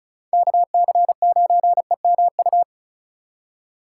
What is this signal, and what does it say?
Morse code, heard as a keyed tone, about 45 wpm, 700 Hz. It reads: KC9EMU